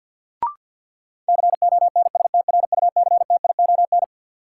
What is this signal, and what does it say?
Morse code, heard as a keyed tone, about 52 wpm, 700 Hz. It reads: CONSTRUCTION
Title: Sentences from Top 1600 Words - New Word Drill - 50wpm